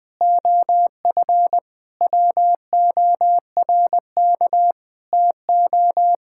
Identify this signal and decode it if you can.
Morse code, heard as a keyed tone, about 20 wpm, 700 Hz. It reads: OF WORK TO
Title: Sentences with 8-Word Phrases - 20wpm